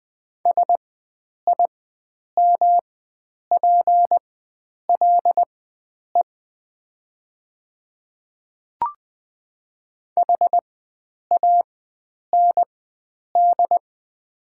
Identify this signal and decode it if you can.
Morse code, heard as a keyed tone, about 20 wpm, spaced as for 9 wpm, 700 Hz. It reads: SIMPLE HAND
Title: 4x Character Spacing - Top 500 Words - 20wpm